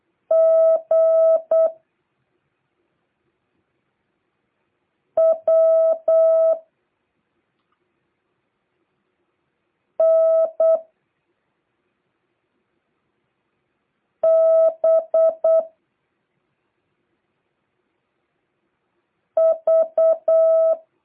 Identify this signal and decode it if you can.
Morse code, heard as a keyed tone, about 8 wpm, 650 Hz. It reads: G W N B V